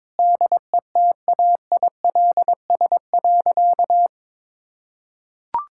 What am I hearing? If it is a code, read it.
Morse code, heard as a keyed tone, about 22 wpm, 700 Hz. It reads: DETAILS.
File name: Sentences with 9-Word Phrases - 22wpm